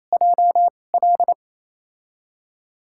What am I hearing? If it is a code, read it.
Morse code, heard as a keyed tone, about 28 wpm, spaced as for 20 wpm, 700 Hz. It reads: JL
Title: USA Call Signs - 5 Characters - 2x Character Spacing - 28wpm